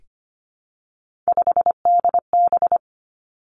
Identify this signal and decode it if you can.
Morse code, heard as a keyed tone, about 25 wpm, 700 Hz. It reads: 5DB
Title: Three Challenging Letter-Number Combinations 25wpm